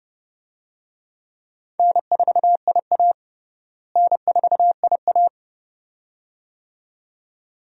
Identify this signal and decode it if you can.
Morse code, heard as a keyed tone, about 30 wpm, 700 Hz. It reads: N4IA N4IA
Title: USA Call Signs - 4 Characters - Repeated 2x - 30wpm